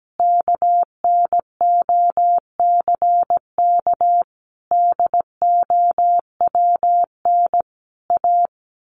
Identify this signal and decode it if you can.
Morse code, heard as a keyed tone, about 17 wpm, 700 Hz. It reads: KNOCK DOWN A
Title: A Sound Of Thunder 17wpm